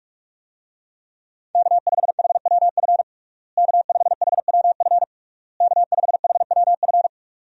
Morse code, heard as a keyed tone, about 46 wpm, 700 Hz. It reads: K5HWF K5HWF K5HWF
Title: USA Call Signs - 5 Characters - Repeated 3x - 45wpm